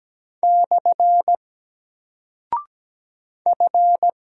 Morse code, heard as a keyed tone, about 17 wpm, 700 Hz. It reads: / F
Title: Single Character (Letter:Number:Punctuation:Prosign) Alt 17wpm